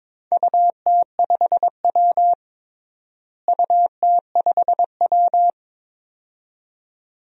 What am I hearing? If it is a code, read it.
Morse code, heard as a keyed tone, about 22 wpm, 700 Hz. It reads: UT5W UT5W
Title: DXCC-100 Call Signs - 4 Characters - Repeated 2x - 22wpm